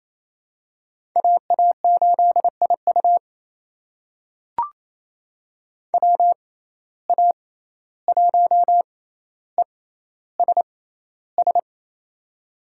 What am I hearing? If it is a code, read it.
Morse code, heard as a keyed tone, about 27 wpm, spaced as for 14 wpm, 700 Hz. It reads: AA8IU WA1ESS
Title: USA Call Signs - 6x Character Spacing - 28wpm